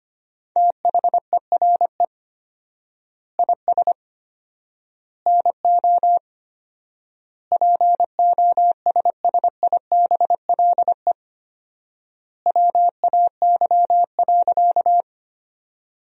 Morse code, heard as a keyed tone, about 25 wpm, 700 Hz. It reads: THERE IS NO POSSIBLE WAY.
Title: 4x Word-Spacing - Sentences from Top 500 Words - 25wpm